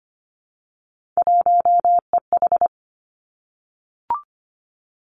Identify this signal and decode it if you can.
Morse code, heard as a keyed tone, about 25 wpm, 700 Hz. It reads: 1EH
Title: Three Challenging Letter-Number Combinations 25wpm